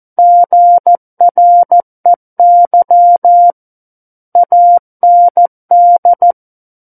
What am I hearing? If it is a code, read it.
Morse code, heard as a keyed tone, about 14 wpm, 700 Hz. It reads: GREY AND
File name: Book One / 05-Chapter 5 - 14 WPM